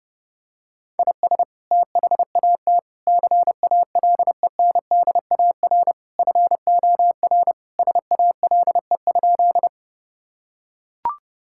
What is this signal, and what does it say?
Morse code, heard as a keyed tone, about 32 wpm, 700 Hz. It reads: IS THAT CALENDAR FOR SALE?
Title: Sentences from Top 1600 Words 30wpm